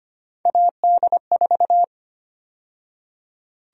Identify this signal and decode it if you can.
Morse code, heard as a keyed tone, about 25 wpm, 700 Hz. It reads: AD4
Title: USA Call Sign Prefixes - 25wpm